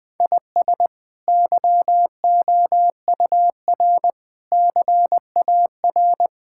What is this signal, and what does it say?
Morse code, heard as a keyed tone, about 20 wpm, 700 Hz. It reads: IS YOUR CAR
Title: Sentences from Top 1600 Words 20wpm